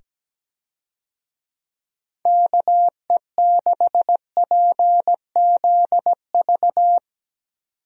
Morse code, heard as a keyed tone, about 17 wpm, 700 Hz. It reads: KE6PZV
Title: USA Call Signs - 6 Characters - 17wpm